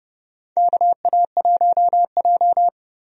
Morse code, heard as a keyed tone, about 30 wpm, 700 Hz. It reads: KA1J
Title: USA Call Signs - 4 Characters - 30wpm